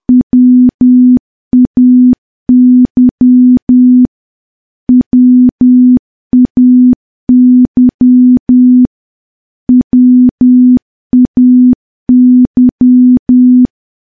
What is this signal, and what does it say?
Morse code, heard as a keyed tone, about 10 wpm, 250 Hz. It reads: WAY WAY WAY
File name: morse / morse-training-10WPM